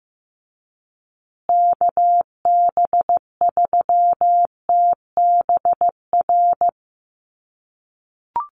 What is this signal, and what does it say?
Morse code, heard as a keyed tone, about 15 wpm, 700 Hz. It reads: KB3TBR
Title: USA Call Signs Repeated 3x 15wpm